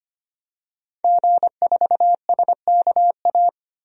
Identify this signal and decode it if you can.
Morse code, heard as a keyed tone, about 25 wpm, 700 Hz. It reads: G4SKA
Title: DXCC-100 Call Signs - 5 Characters - 25wpm